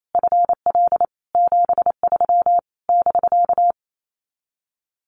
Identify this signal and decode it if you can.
Morse code, heard as a keyed tone, about 28 wpm, 700 Hz. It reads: FL 73 <BK>
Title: Common QSO Phrases Encore 28wpm